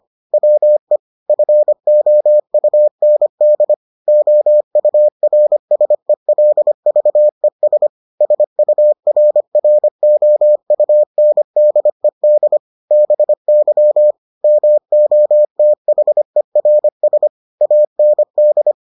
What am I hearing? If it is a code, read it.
Morse code, heard as a keyed tone, about 25 wpm, 600 Hz. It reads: WE FOUND OURSELVES SURROUNDED BY MOTHERS AND